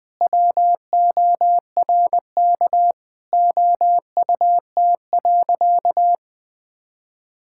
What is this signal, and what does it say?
Morse code, heard as a keyed tone, about 20 wpm, 700 Hz. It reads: WORK OUT.
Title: Sentences with 3-Word Phrases - 20wpm